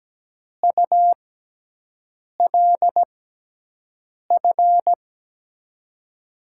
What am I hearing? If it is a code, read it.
Morse code, heard as a keyed tone, about 17 wpm, spaced as for 6 wpm, 700 Hz. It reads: ULF
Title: USA Call Signs - 6 Characters - 6x Character Spacing - 17wpm